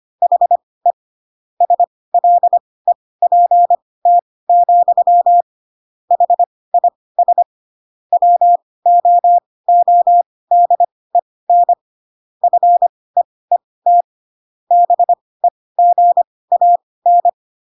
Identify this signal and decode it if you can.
Morse code, heard as a keyed tone, about 25 wpm, spaced as for 18 wpm, 700 Hz. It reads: HE SLEPT, HIS WOODEN FEET BEGAN
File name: Pinocchio / Pinocchio ch6 12 25 ik1hge